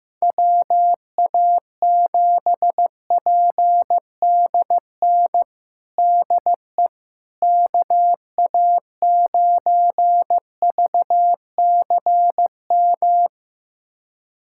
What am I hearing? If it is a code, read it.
Morse code, heard as a keyed tone, about 15 wpm, 700 Hz. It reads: WA7PDN DE KA9VCM
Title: USA Callsign DE Callsign 15wpm